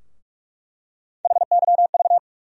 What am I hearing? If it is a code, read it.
Morse code, heard as a keyed tone, about 46 wpm, 700 Hz. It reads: HYV
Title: USA Call Sign Postfixes - 3 Letters 45wpm